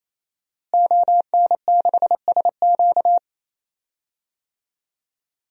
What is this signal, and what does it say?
Morse code, heard as a keyed tone, about 27 wpm, 700 Hz. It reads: ON6SQ
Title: DXCC-100 Call Signs - 5 Characters - Repeated 3x - 28wpm